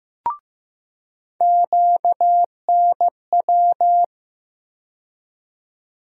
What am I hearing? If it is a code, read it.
Morse code, heard as a keyed tone, about 15 wpm, 700 Hz. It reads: QNW
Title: USA Call Sign Postfixes - 3 Letters 15wpm